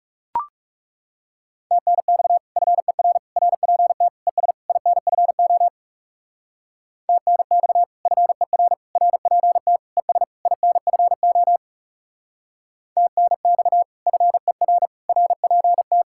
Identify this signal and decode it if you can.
Morse code, heard as a keyed tone, about 36 wpm, 700 Hz. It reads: TNX FER RPT ES INFO TNX FER RPT ES INFO TNX FER RPT
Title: Speed-Racing Common QSO Phrases Encore - 30wpm